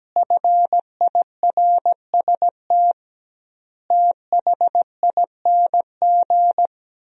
Morse code, heard as a keyed tone, about 17 wpm, 700 Hz. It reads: FIRST THING
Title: 2x Word-Spacing - Sentences from Top 300 Words - 17wpm